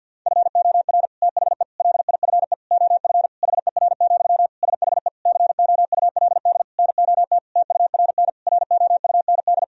Morse code, heard as a keyed tone, about 50 wpm, 700 Hz. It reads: FOR THE LIFE OF HER, SHE COULD NOT TURN ROUND
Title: Sentences from Top 1000 Words - Review Part 2 - 50wpm